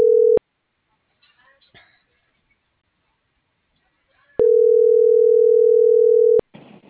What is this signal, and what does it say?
Ringback tone (North America), 440 + 480 Hz.
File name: CENTRAL MULTIMODAL / out josimar.cruz